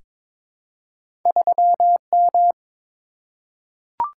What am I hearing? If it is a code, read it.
Morse code, heard as a keyed tone, about 22 wpm, 700 Hz. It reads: 3M
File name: practice / 22wpm / Two Letter-Number Combinations - 22wpm